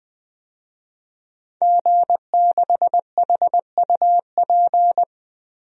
Morse code, heard as a keyed tone, about 20 wpm, 700 Hz. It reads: G6HUP
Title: DXCC-100 Call Signs - 5 Characters - Repeated 2x - 20wpm